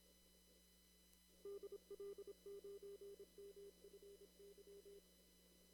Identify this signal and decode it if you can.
Morse code, heard as a keyed tone, about 26 wpm, 400 Hz. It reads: DL9MFY
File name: CW / kt5j wpx cw-op-k5pi